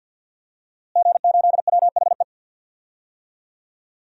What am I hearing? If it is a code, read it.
Morse code, heard as a keyed tone, about 51 wpm, 700 Hz. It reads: G8WHE
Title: DXCC-100 Call Signs - 5 Characters - Repeated 2x - 50wpm